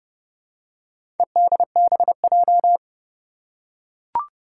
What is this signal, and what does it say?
Morse code, heard as a keyed tone, about 32 wpm, 700 Hz. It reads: EDBJ
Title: Four Challenging Letter-Number Combinations 30wpm